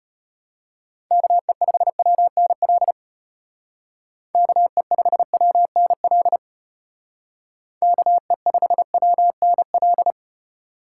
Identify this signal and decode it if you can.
Morse code, heard as a keyed tone, about 30 wpm, 700 Hz. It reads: KE5WNL KE5WNL KE5WNL
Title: Speed Racing - USA Call Signs 25wpm